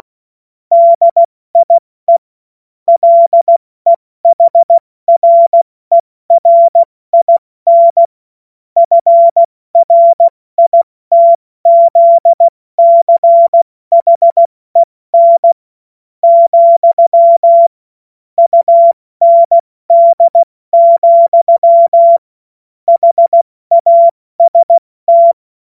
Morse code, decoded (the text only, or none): DIE LEHRERIN FRITZCHEN, UND, HAST